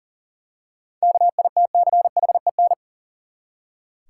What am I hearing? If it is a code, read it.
Morse code, heard as a keyed tone, about 37 wpm, 700 Hz. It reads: KITCHEN
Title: Seven Letter Words - 40wpm